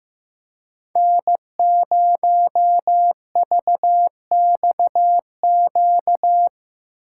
Morse code, heard as a keyed tone, about 15 wpm, 700 Hz. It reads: N0VXQ
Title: USA Call Signs - 5 Characters - 2x Character Spacing - 15wpm